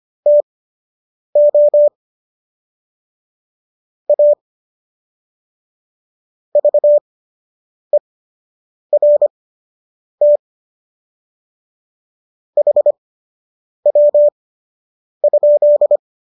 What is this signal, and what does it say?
Morse code, heard as a keyed tone, about 25 wpm, spaced as for 8 wpm, 590 Hz. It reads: TO A VERT HW?
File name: qso006 08